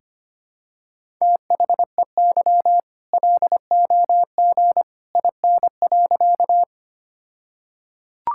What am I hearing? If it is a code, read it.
Morse code, heard as a keyed tone, about 25 wpm, 700 Hz. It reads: THEY LOG IN.
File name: practice / 25wpm / All-Age Appropriate Jokes - 25wpm